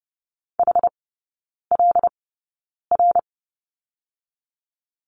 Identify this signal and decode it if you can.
Morse code, heard as a keyed tone, about 30 wpm, 700 Hz. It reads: H L R
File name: Lesson 19e - Letter w - Three Characters 30wpm